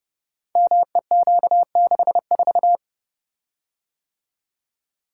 Morse code, heard as a keyed tone, about 30 wpm, 700 Hz. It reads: MEQ64
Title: Code Groups - 5 Letter:Numbers - 30wpm